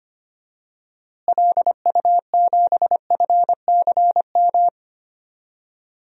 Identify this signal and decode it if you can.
Morse code, heard as a keyed tone, about 25 wpm, 700 Hz. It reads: LU7FCM
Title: DXCC-100 Call Signs - 6 Characters - Repeated 2x - 25wpm